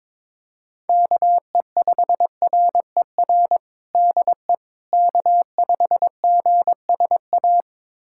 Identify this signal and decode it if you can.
Morse code, heard as a keyed tone, about 22 wpm, 700 Hz. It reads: KE5RER DE K5GSA
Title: USA Callsign DE Callsign 22wpm